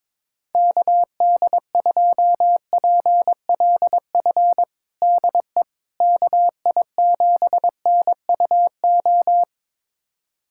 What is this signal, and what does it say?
Morse code, heard as a keyed tone, about 22 wpm, 700 Hz. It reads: KD2PLF DE KI7NUO
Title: USA Callsign DE Callsign - 6 Characters - 22wpm